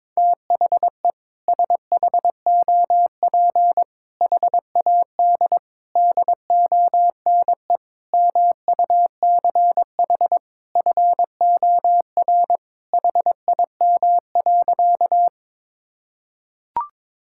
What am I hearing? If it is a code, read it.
Morse code, heard as a keyed tone, about 22 wpm, 700 Hz. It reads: THE SHOP HAD DONE MUCH FOR HIM.
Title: Sentences from Top 1600 Words 22wpm